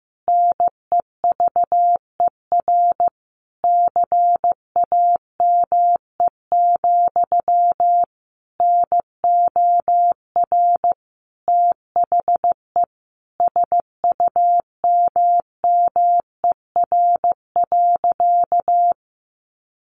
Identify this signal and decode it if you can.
Morse code, heard as a keyed tone, about 15 wpm, 700 Hz. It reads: NEVER CAME, NOR THE SUMMER.
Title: Sentences from Top 1500 Words 15wpm